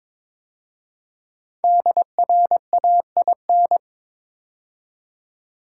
Morse code, heard as a keyed tone, about 22 wpm, 700 Hz. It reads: DRAIN
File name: 5 Letter Words - First 3 Letters Given - 22wpm